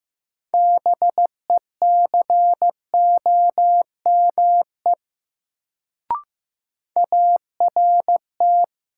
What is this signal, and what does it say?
Morse code, heard as a keyed tone, about 15 wpm, 700 Hz. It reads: BECOME ART
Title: Sentences from Top 600 Words - New Word Drill - 15wpm